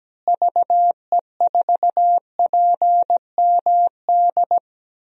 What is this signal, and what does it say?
Morse code, heard as a keyed tone, about 17 wpm, 700 Hz. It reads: VE4PMD
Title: DXCC-100 Call Signs - 6 Characters - Repeated 3x - 17wpm